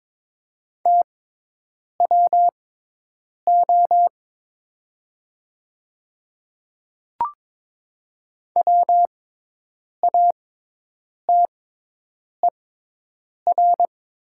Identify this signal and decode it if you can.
Morse code, heard as a keyed tone, about 22 wpm, spaced as for 8 wpm, 700 Hz. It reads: TWO WATER
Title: 6x Character Spacing - Top 200 Words - 22wpm